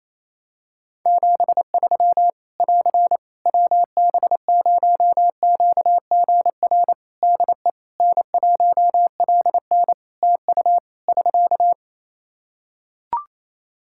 Morse code, heard as a keyed tone, about 28 wpm, 700 Hz. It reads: 73 <AR> WB0QGR DE N1LN TU <SK>
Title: Practice QSO - Exchange 4 - Ending - 28wpm